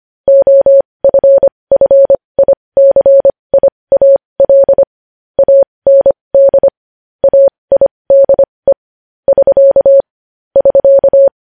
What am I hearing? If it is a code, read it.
Morse code, heard as a keyed tone, about 25 wpm, spaced as for 20 wpm, 550 Hz. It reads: OFFICIAL AND AIDE <SK> <SK>
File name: podcast / News headlines-1x-25x20